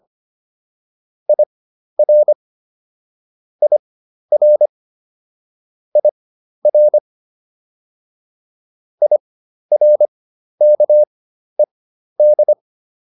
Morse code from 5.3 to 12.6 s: IR IRKED

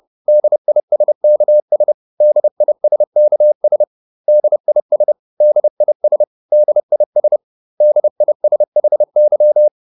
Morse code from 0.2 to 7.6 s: DISKS DISKS DIS DIS DIS

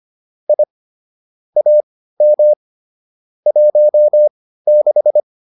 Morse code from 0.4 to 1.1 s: I

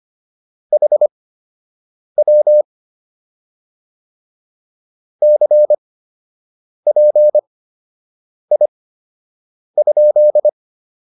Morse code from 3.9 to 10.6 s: CPI?

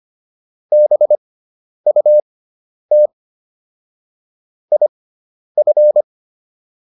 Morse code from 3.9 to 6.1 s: IF

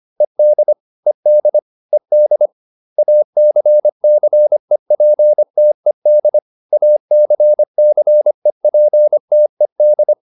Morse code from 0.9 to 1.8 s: ED

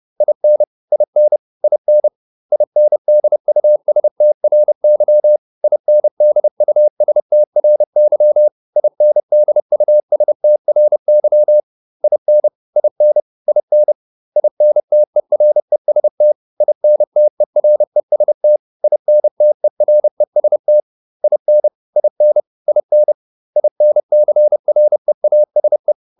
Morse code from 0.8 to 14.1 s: IN IN INDUSTRY INDUSTRY INDUSTRY IN IN IN

